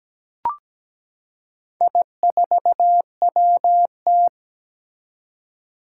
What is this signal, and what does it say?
Morse code, heard as a keyed tone, about 17 wpm, 700 Hz. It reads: I4WT
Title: Four Challenging Letter-Number Combinations 17wpm